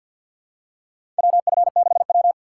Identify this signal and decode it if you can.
Morse code, heard as a keyed tone, about 50 wpm, 700 Hz. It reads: WF6W
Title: USA Call Signs - 4 Characters - 2x Character Spacing - 50wpm